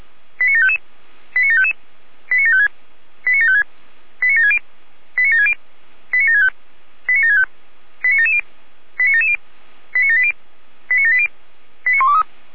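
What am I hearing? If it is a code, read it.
Five-tone selective call (ZVEI), heard as quick runs of five tones, tones of about 80 ms. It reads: 89755 89755 89756 89756 89760 89760 89765 89765 89770 89770 89780 89780 89123
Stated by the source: ZVEI five-tone call 89720 received by SDR.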